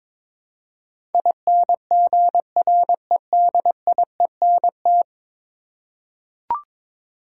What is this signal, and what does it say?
Morse code, heard as a keyed tone, about 22 wpm, 700 Hz. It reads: INGREDIENT
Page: morse-code-ninja-practice-22wpm